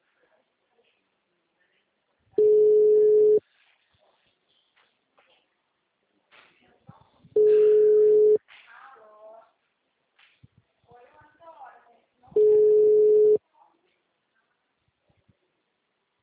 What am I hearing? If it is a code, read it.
Ringback tone (Europe, 425 Hz), with speech.
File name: out- -2011- - - .59420